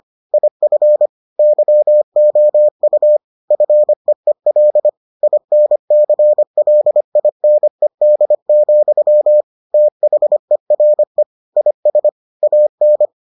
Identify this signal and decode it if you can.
Morse code, heard as a keyed tone, about 25 wpm, 600 Hz. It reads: IF YOU FEEL INCLINED, THERE IS AN